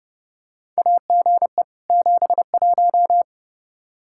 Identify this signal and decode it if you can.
Morse code, heard as a keyed tone, about 30 wpm, 700 Hz. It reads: AGE 71
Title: Practice QSO - Exchange 3 - More Info - 30wpm